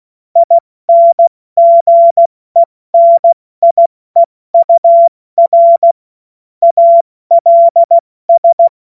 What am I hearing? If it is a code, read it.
Morse code, heard as a keyed tone, about 16 wpm, 680 Hz. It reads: INGENIEUR ALS
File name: Morsekurs / Witze / Witz 07. Dezember 2021, 70 F80